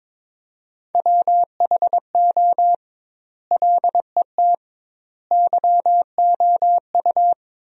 Morse code, heard as a keyed tone, about 22 wpm, 700 Hz. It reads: WHO LET YOU
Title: 2x Word-Spacing - Sentences from Top 300 Words - 22wpm